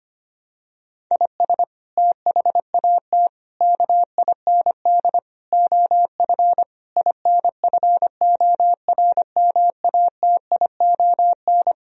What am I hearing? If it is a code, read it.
Morse code, heard as a keyed tone, about 25 wpm, 700 Hz. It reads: IS THAT KIND OF INFORMATION
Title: Sentences from Top 1000 Words - Review Part 3 - 25wpm